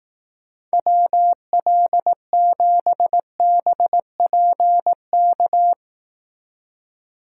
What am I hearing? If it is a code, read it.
Morse code, heard as a keyed tone, about 18 wpm, 700 Hz. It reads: WL7BPK
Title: Speed Racing - USA 2x3 Call Signs 15wpm